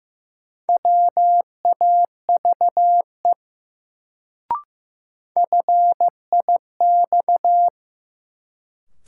Morse code, heard as a keyed tone, about 15 wpm, 700 Hz. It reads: WAVE FIX
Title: Sentences from Top 1400 Words - New Word Drill - 15wpm